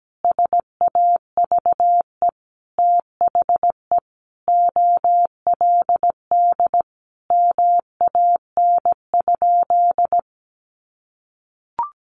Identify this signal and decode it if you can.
Morse code, heard as a keyed tone, about 17 wpm, 700 Hz. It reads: SAVE THE OLD MAN?
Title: Sentences from Top 1100 Words 17wpm